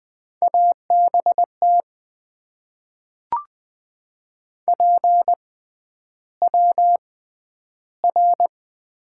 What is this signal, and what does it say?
Morse code, heard as a keyed tone, about 20 wpm, spaced as for 10 wpm, 700 Hz. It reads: ABT PWR
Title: Top 90 QSO Elements - 6x Character Spacing - 20wpm